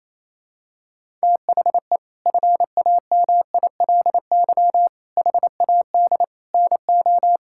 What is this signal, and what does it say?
Morse code, heard as a keyed tone, about 28 wpm, 700 Hz. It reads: THE FAMILY HAD NO
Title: Sentences from Top 1600 Words 28wpm